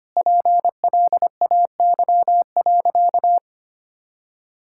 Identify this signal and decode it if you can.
Morse code, heard as a keyed tone, about 25 wpm, 700 Hz. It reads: PLAY.
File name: practice / 25wpm / Difficult Sentences with Hints - 25wpm